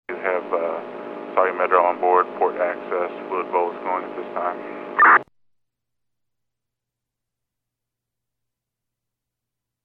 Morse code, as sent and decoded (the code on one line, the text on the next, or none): none